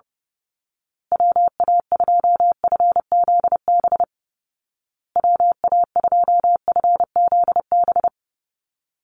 Morse code, heard as a keyed tone, about 30 wpm, 700 Hz. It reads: WA2FZB WA2FZB
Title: Call Signs Repeated 2x - 30wpm